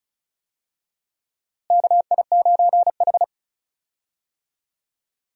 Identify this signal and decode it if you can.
Morse code, heard as a keyed tone, about 36 wpm, 700 Hz. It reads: KI9H
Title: USA Call Signs - 4 Characters - Repeated 2x - 35wpm